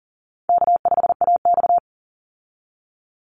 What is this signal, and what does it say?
Morse code, heard as a keyed tone, about 38 wpm, 700 Hz. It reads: K5AX
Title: Easy Call Signs - 40wpm